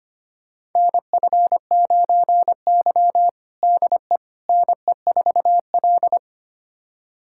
Morse code, heard as a keyed tone, about 25 wpm, 700 Hz. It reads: NF9Y DE NE4L
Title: USA Callsign DE Callsign - 4 Characters - 25wpm